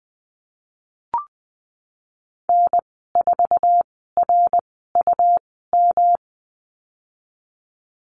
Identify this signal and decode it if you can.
Morse code, heard as a keyed tone, about 20 wpm, spaced as for 15 wpm, 700 Hz. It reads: N4RUM
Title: 140 Lesson 23g - Letter f - Call Signs 20wpm